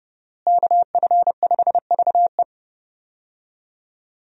Morse code, heard as a keyed tone, about 30 wpm, 700 Hz. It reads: KF5VE
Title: USA Call Signs - 5 Characters - Repeated 2x - 30wpm